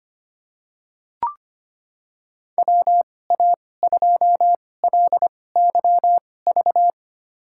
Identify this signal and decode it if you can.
Morse code, heard as a keyed tone, about 25 wpm, spaced as for 18 wpm, 700 Hz. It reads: WA2LYV
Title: USA Call Signs - 6 Characters - 2x Character Spacing - 25wpm